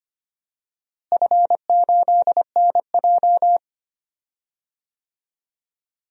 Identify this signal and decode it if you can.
Morse code, heard as a keyed tone, about 25 wpm, 700 Hz. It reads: F8NJ
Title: DXCC-100 Call Signs - 4 Characters - Repeated 3x - 25wpm